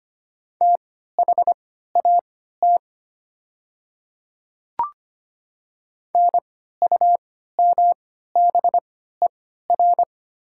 Morse code, heard as a keyed tone, about 25 wpm, spaced as for 14 wpm, 700 Hz. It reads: THAT NUMBER
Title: 3x Character Spacing - Top 100 Words - 25wpm